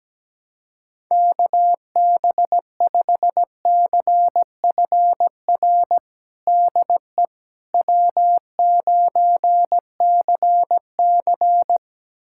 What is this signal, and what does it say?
Morse code, heard as a keyed tone, about 17 wpm, 700 Hz. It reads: KB5CFR DE W9CC
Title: USA Callsign DE Callsign 17wpm